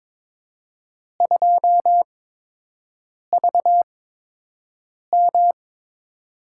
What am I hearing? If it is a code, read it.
Morse code, heard as a keyed tone, about 23 wpm, spaced as for 6 wpm, 700 Hz. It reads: 2VM
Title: USA Call Signs - 8x Character Spacing - 22wpm